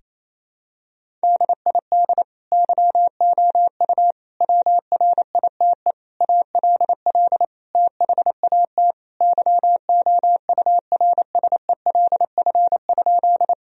Morse code, heard as a keyed tone, about 28 wpm, 700 Hz. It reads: DID YOU WRITE ALL THAT YOURSELF?
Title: Sentences from Top 1000 Words - Review Part 1 - 28wpm